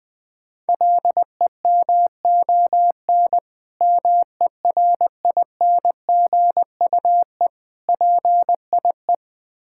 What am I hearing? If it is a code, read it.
Morse code, heard as a keyed tone, about 21 wpm, 700 Hz. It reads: LEMON MERINGUE PIE
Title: Difficult Words with Hints - 20wpm